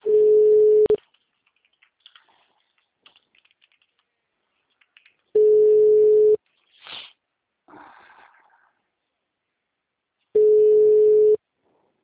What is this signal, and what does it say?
Ringback tone (Europe, 425 Hz).